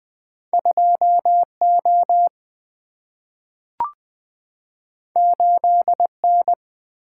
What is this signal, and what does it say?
Morse code, heard as a keyed tone, about 20 wpm, 700 Hz. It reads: 2O 8N
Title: Single-Letter Single-Number 20wpm